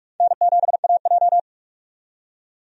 Morse code, heard as a keyed tone, about 46 wpm, 700 Hz. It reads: N7AJ
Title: USA Call Signs - 4 Characters - 45wpm